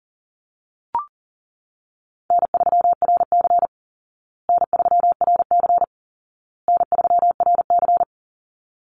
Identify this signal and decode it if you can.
Morse code, heard as a keyed tone, about 37 wpm, 700 Hz. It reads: N3RC N3RC N3RC
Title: Call Signs Repeated 2x - 40wpm